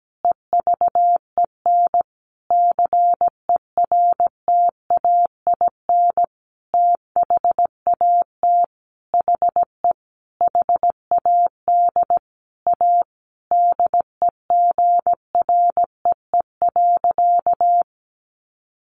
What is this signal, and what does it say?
Morse code, heard as a keyed tone, about 17 wpm, 700 Hz. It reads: EVEN CERTAIN THAT HE HAD A DEGREE.
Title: Sentences from Top 1500 Words 17wpm